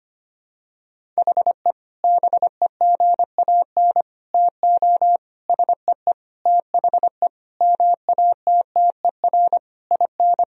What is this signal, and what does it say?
Morse code, heard as a keyed tone, about 25 wpm, 700 Hz. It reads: HE BEGAN TO SEE THE MATTER IN
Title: Sentences from Top 1000 Words - Review Part 1 - 25wpm